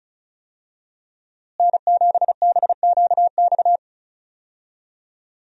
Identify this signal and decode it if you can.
Morse code, heard as a keyed tone, about 34 wpm, 700 Hz. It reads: N7BQX
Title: USA Call Signs - 5 Characters - Repeated 3x - 35wpm